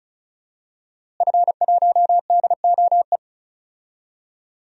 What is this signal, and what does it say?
Morse code, heard as a keyed tone, about 34 wpm, 700 Hz. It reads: F1DOE
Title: DXCC-100 Call Signs - 5 Characters - Repeated 2x - 35wpm